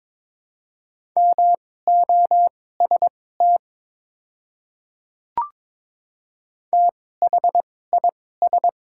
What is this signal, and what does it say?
Morse code, heard as a keyed tone, about 22 wpm, spaced as for 16 wpm, 700 Hz. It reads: MOST THIS
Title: 2x Character Spacing - Top 100 Words - 22wpm